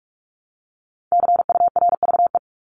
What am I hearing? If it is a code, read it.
Morse code, heard as a keyed tone, about 44 wpm, 700 Hz. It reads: CURVE
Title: Top 1501-1750 Words 45wpm